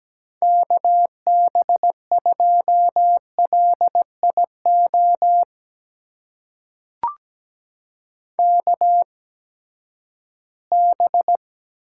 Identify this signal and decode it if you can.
Morse code, heard as a keyed tone, about 17 wpm, spaced as for 12 wpm, 700 Hz. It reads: KB2LIO KB
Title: USA Call Signs - 6 Characters - 8x Character Spacing - 17wpm